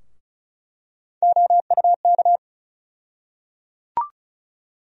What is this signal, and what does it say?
Morse code, heard as a keyed tone, about 36 wpm, 700 Hz. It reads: OUK
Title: USA Call Sign Postfixes - 3 Letters 35wpm